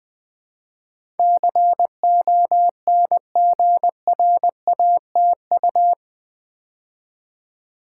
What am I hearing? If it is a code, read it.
Morse code, heard as a keyed tone, about 20 wpm, 700 Hz. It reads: CONGRATU
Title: Build A Word - 4 Letters At A Time - 12-Letter Words - 20wpm